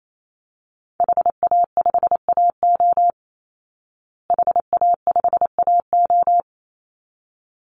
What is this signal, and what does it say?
Morse code, heard as a keyed tone, about 28 wpm, 700 Hz. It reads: HA5AO HA5AO
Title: Call Signs Repeated 3x - 28wpm